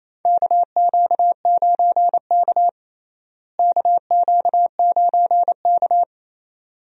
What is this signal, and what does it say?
Morse code, heard as a keyed tone, about 28 wpm, 700 Hz. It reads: KQ9K KQ9K
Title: USA Call Signs - 4 Characters - Repeated 2x - 28wpm